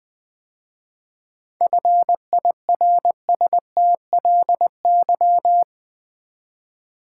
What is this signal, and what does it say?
Morse code, heard as a keyed tone, about 20 wpm, 700 Hz. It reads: FIRSTLY
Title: 7 Letter Words - First 3 Letters Given - 20wpm